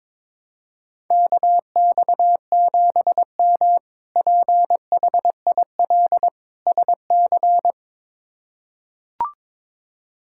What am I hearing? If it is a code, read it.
Morse code, heard as a keyed tone, about 21 wpm, 700 Hz. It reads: KX7M PHIL SC
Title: K1USN SST QSO Practice - 22wpm